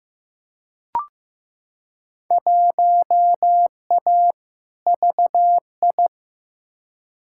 Morse code, heard as a keyed tone, about 15 wpm, 700 Hz. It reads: 1A VI
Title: Field Day Exchanges Encore - 15wpm